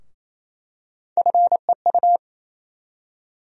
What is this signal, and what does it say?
Morse code, heard as a keyed tone, about 28 wpm, 700 Hz. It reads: FEU
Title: USA Call Sign Postfixes - 3 Letters 28wpm